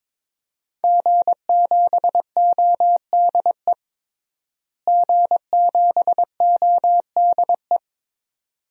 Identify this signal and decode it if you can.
Morse code, heard as a keyed tone, about 21 wpm, 700 Hz. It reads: G7ODE G7ODE
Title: DXCC-100 Call Signs - 5 Characters - Repeated 2x - 22wpm